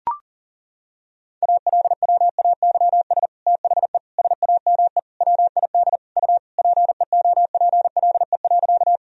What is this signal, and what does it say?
Morse code, heard as a keyed tone, about 40 wpm, 700 Hz. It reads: ALWAYS THE SAME WID U PEOPLE.
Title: Sentences from Top 300 Words and Abbreviations 40wpm